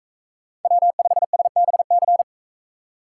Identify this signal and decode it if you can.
Morse code, heard as a keyed tone, about 44 wpm, 700 Hz. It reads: W5SBC